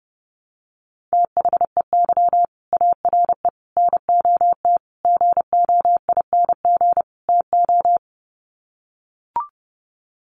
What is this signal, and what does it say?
Morse code, heard as a keyed tone, about 30 wpm, 700 Hz. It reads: THEY ARE NOT GOING TO
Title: Sets of 5 Words Encore2 30wpm